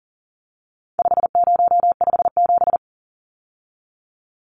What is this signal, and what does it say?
Morse code, heard as a keyed tone, about 40 wpm, 700 Hz. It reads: 5057
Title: Numbers - 4-Digit 40wpm